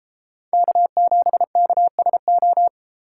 Morse code, heard as a keyed tone, about 31 wpm, 700 Hz. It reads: K7KSO